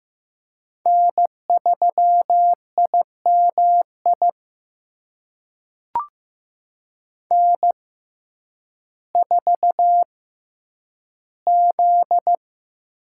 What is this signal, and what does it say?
Morse code, heard as a keyed tone, about 15 wpm, spaced as for 9 wpm, 700 Hz. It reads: N3IMI N4Z
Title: USA Call Signs - 5 Characters - 6x Character Spacing - 15wpm